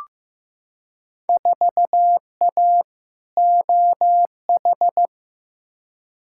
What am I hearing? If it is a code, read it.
Morse code, heard as a keyed tone, about 15 wpm, 700 Hz. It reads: 4A OH
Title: Field Day Exchanges - 15wpm